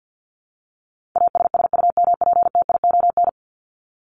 Morse code, heard as a keyed tone, about 50 wpm, 700 Hz. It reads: ASSUMPTION
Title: Top 1751-2000 Words 50wpm